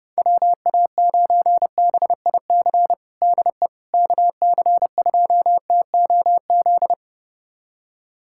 Morse code, heard as a keyed tone, about 30 wpm, 700 Hz. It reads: WA9BIC DE KC2TOZ
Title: USA Callsign DE Callsign 30wpm